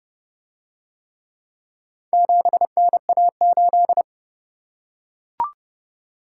Morse code, heard as a keyed tone, about 28 wpm, 700 Hz. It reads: 7NA8